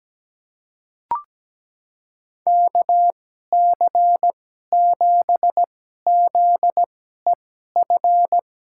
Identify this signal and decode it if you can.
Morse code, heard as a keyed tone, about 17 wpm, spaced as for 12 wpm, 700 Hz. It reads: KC7ZEF
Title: USA Call Signs - 6 Characters - 2x Character Spacing - 17wpm